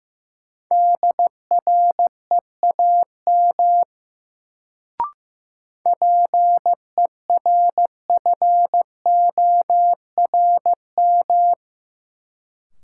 Morse code, heard as a keyed tone, about 15 wpm, 700 Hz. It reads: DREAM PERFORM
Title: Sentences from Top 1500 Words - New Word Drill Alt - 15wpm